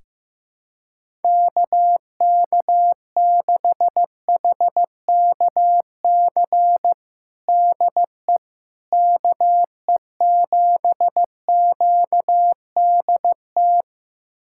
Morse code, heard as a keyed tone, about 16 wpm, 700 Hz. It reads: KK6HKC DE KE7QDT
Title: USA Callsign DE Callsign - 6 Characters - 15wpm